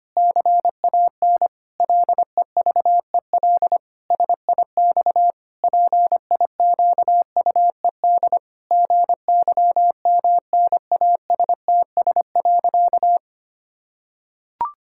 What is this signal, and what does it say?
Morse code, heard as a keyed tone, about 25 wpm, 700 Hz. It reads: CAN LEVEL SIX PIQUED GYMNASTS.
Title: Pangram Challenge v2 - 25wpm